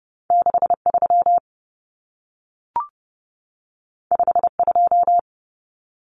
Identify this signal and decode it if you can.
Morse code, heard as a keyed tone, about 30 wpm, 700 Hz. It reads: 63 52
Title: Numbers - 2-Digit 30wpm